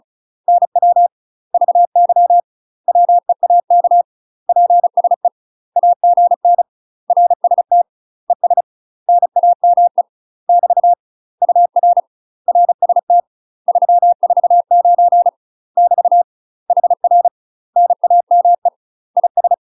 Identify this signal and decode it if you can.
Morse code, heard as a keyed tone, about 35 wpm, 700 Hz. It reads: NW VY WEAK PSE AGN RST ES NAME <BT> UR RST 349 <BT> HR NAME IS
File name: UK MASTER QSOs TRAINING / 35WPM / UK MASTER QSOs 0038